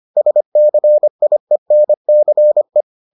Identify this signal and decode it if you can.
Morse code, heard as a keyed tone, about 25 wpm, 600 Hz. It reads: SCIENCE